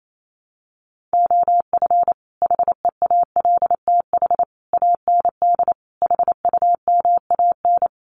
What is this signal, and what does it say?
Morse code, heard as a keyed tone, about 28 wpm, 700 Hz. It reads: OF HEALTH AND HUMAN
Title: Sets of 5 Words 28wpm